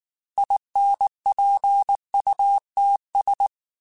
Morse code, heard as a keyed tone, about 19 wpm, 800 Hz. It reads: INPUTS
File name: Technical Subjects / Technical Subjects-mp3s / AND Gate @18